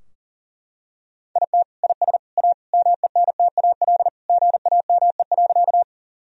Morse code, heard as a keyed tone, about 40 wpm, 700 Hz. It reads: IT IS A MENTAL GAME.
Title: Sentences from Top 1000 Words - Review Part 4 - 40wpm